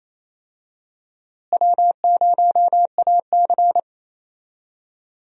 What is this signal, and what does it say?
Morse code, heard as a keyed tone, about 28 wpm, 700 Hz. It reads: W0AC
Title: USA Call Signs - 4 Characters - Repeated 2x - 28wpm